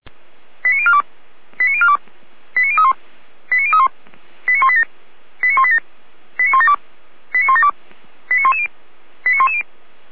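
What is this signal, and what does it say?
Five-tone selective call (ZVEI), heard as quick runs of five tones, tones of about 70 ms. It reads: 89042 89042 89031 89031 89187 89187 89172 89172 89110 89110
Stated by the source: ZVEI five-tone selective call received on SDR, Feuerwehr Orlamünde (fire brigade).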